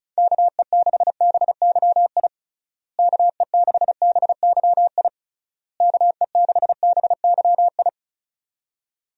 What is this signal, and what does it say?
Morse code, heard as a keyed tone, about 36 wpm, 700 Hz. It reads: KE6BYI KE6BYI KE6BYI
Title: USA Call Signs - 6 Characters - Repeated 3x - 35wpm